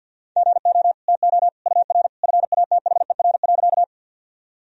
Morse code, heard as a keyed tone, about 52 wpm, 700 Hz. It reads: GO TO UR FATHER.
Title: Sentences from Top 300 Words and Abbreviations 50wpm